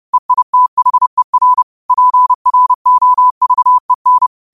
Morse code, heard as a keyed tone, about 30 wpm, 1000 Hz. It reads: EITHER PROVEN
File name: qotdfiles / 30wpm